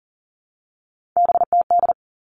Morse code, heard as a keyed tone, about 40 wpm, 700 Hz. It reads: BTD
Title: All Cut Numbers - 3-Digit 40wpm